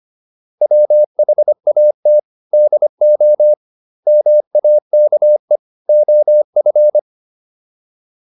Morse code, heard as a keyed tone, about 25 wpm, 600 Hz. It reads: WHAT DO MAKE OF